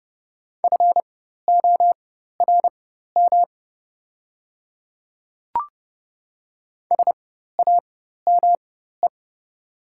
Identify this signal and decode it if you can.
Morse code, heard as a keyed tone, about 31 wpm, spaced as for 14 wpm, 690 Hz. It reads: FORM SAME
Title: 4x Character Spacing - Top 300 Words - 30wpm